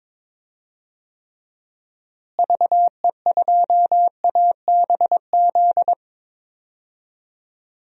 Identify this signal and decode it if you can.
Morse code, heard as a keyed tone, about 22 wpm, 700 Hz. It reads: VE2ABZ
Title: DXCC-100 Call Signs - 6 Characters - Repeated 2x - 22wpm